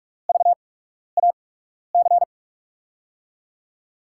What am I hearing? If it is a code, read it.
Morse code, heard as a keyed tone, about 44 wpm, spaced as for 12 wpm, 700 Hz. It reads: VAC
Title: USA Call Signs - 5 Characters - 8x Character Spacing - 45wpm